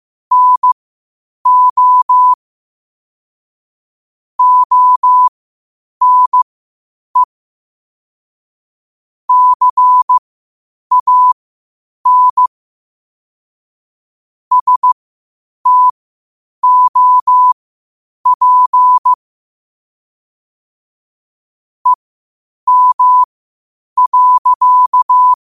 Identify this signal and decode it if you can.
Morse code, heard as a keyed tone, about 15 wpm, spaced as for 8 wpm, 1000 Hz. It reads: NO ONE CAN STOP EM.
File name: qotdfiles / 8wpm